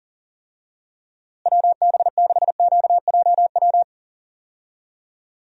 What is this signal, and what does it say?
Morse code, heard as a keyed tone, about 40 wpm, 700 Hz. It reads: WB6QJW
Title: USA Call Signs - 6 Characters - Repeated 3x - 40wpm